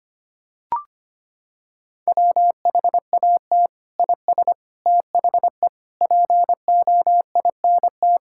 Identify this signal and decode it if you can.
Morse code, heard as a keyed tone, about 25 wpm, 700 Hz. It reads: WHAT IS THE POINT
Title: Sentences from Top 1000 Words - Review Part 5 - 25wpm